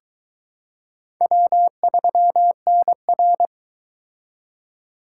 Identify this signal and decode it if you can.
Morse code, heard as a keyed tone, about 22 wpm, 700 Hz. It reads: W3NR